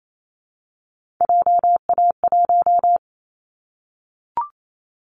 Morse code, heard as a keyed tone, about 28 wpm, 700 Hz. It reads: JA1
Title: Callsign Prefixes for Top 50 DXCC - 28wpm